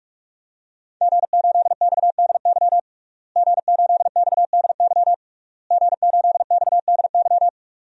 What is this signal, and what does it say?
Morse code, heard as a keyed tone, about 46 wpm, 700 Hz. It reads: G8XDY G8XDY G8XDY
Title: DXCC-100 Call Signs - 5 Characters - Repeated 3x - 45wpm